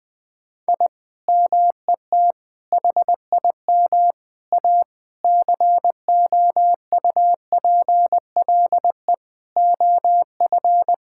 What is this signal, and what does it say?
Morse code, heard as a keyed tone, about 21 wpm, 700 Hz. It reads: I MET HIM A COUPLE OF
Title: Sentences with 5-Word Phrases - 20wpm